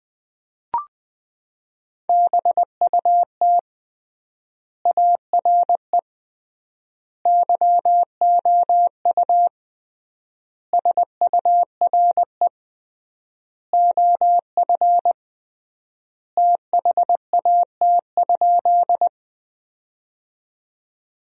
Morse code, heard as a keyed tone, about 21 wpm, 700 Hz. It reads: BUT ARE YOU SURE OF THAT?
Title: 3x Word-Spacing - Sentences from Top 300 Words - 20wpm